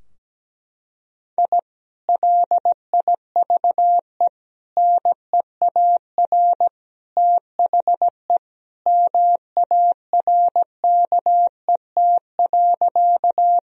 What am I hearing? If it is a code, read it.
Morse code, heard as a keyed tone, about 17 wpm, 700 Hz. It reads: I LIVE NEAR THE MARKET.
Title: Sentences from Top 1000 Words - Review Part 3 - 17wpm